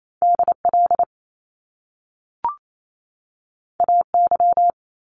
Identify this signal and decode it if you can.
Morse code, heard as a keyed tone, about 28 wpm, 700 Hz. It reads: DL AY
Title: Top Two Letter Combinations in Words 28wpm